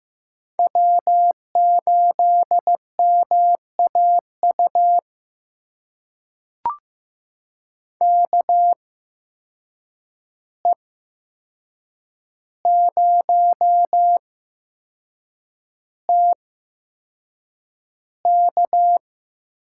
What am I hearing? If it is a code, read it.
Morse code, heard as a keyed tone, about 15 wpm, spaced as for 6 wpm, 700 Hz. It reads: W8MAU KE0TK
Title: USA Call Signs - 5 Characters - 8x Character Spacing - 15wpm